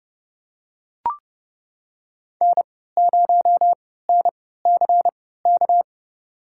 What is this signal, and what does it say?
Morse code, heard as a keyed tone, about 29 wpm, spaced as for 17 wpm, 700 Hz. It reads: N0NCK
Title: USA Call Signs - 5 Characters - 3x Character Spacing - 30wpm